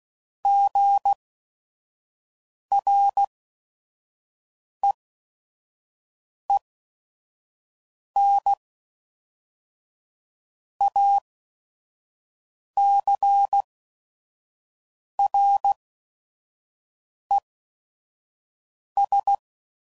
Morse code, heard as a keyed tone, about 16 wpm, spaced as for 5 wpm, 800 Hz. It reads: GREENACRES
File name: CW Code Groups @5WPM / fromSM's / Cities